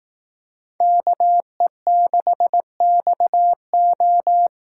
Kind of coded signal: Morse code

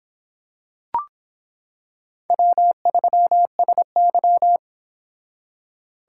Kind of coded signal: Morse code

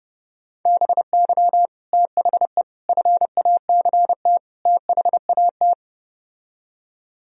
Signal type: Morse code

